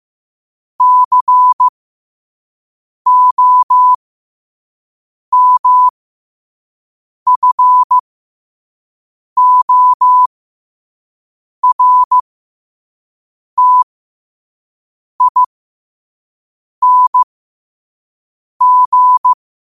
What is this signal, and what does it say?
Morse code, heard as a keyed tone, about 15 wpm, spaced as for 5 wpm, 1000 Hz. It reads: COMFORTING
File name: qotdfiles / 5wpm